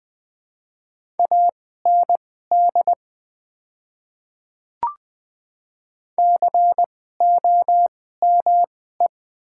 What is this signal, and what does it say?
Morse code, heard as a keyed tone, about 20 wpm, spaced as for 14 wpm, 700 Hz. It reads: AND COME
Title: 2x Character Spacing - Top 500 Words - 20wpm